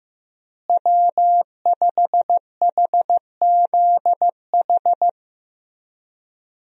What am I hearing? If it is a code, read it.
Morse code, heard as a keyed tone, about 15 wpm, 700 Hz. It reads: W5HZH
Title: USA Call Signs - 5 Characters - Repeated 3x - 15wpm